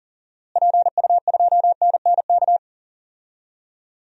Morse code, heard as a keyed tone, about 38 wpm, 700 Hz. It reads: PU2NNK
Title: DXCC-100 Call Signs - 6 Characters - Repeated 2x - 40wpm